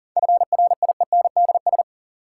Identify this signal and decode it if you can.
Morse code, heard as a keyed tone, about 37 wpm, 700 Hz. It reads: FRIENDS
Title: Sentences from Top 1600 Words - New Word Drill Alt - 40wpm